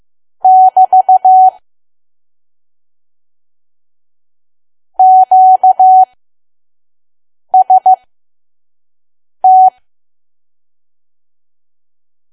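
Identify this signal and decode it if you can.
Morse code, heard as a keyed tone, about 15 wpm, spaced as for 5 wpm, 750 Hz. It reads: <BT> QST
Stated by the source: Morse code practice at 5 WPM from KQ2P.COM.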